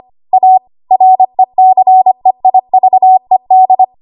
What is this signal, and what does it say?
Morse code, heard as a keyed tone, about 25 wpm, 750 Hz. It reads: A RECEIVED
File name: Archive / 25 WPM / 25WPM